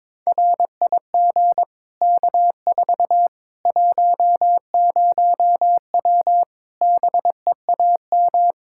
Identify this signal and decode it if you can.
Morse code, heard as a keyed tone, about 22 wpm, 700 Hz. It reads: RIG K4 10W BEAM
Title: K1USN SST QSO Practice 22wpm